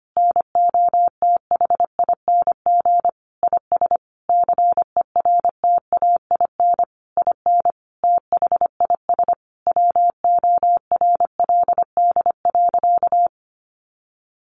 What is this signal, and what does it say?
Morse code, heard as a keyed tone, about 25 wpm, 700 Hz. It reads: NOTHING IS CERTAIN IN THIS WORLD.
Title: Sentences from Top 600 Words 25wpm